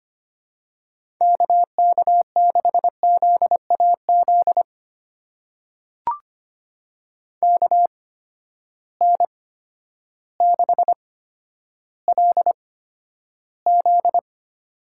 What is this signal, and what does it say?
Morse code, heard as a keyed tone, about 25 wpm, spaced as for 11 wpm, 700 Hz. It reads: KK6ZAZ KN6LZ
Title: USA Call Signs - 8x Character Spacing - 25wpm